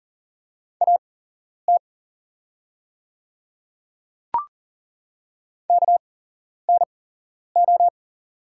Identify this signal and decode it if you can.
Morse code, heard as a keyed tone, about 40 wpm, spaced as for 11 wpm, 690 Hz. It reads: AT KNO
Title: 8x Character Spacing - Top 100 Words - 40wpm